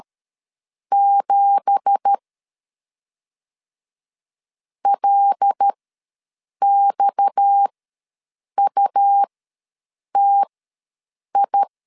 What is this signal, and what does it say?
Morse code, heard as a keyed tone, about 12 wpm, spaced as for 7 wpm, 800 Hz. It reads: 7 LXUTI
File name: LEZIONI / 9 2867 ws 12wpm 3431